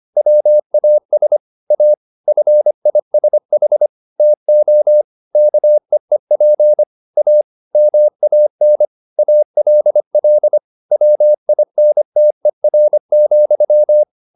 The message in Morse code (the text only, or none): WAS A FISH TO KEEP A MAN ALL WINTER,